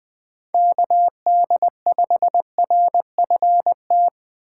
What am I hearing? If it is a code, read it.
Morse code, heard as a keyed tone, about 19 wpm, 700 Hz. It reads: KD5RFT